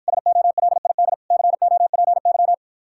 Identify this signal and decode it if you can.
Morse code, heard as a keyed tone, about 54 wpm, 700 Hz. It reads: SOLID COPY